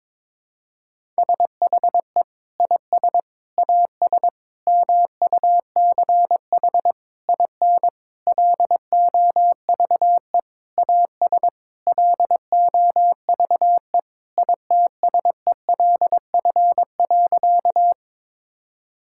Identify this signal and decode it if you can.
Morse code, heard as a keyed tone, about 22 wpm, 700 Hz. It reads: SHE IS AS MUCH IN LOVE AS LOVE ITSELF.
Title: Sentences from Top 1000 Words - Review Part 5 - 22wpm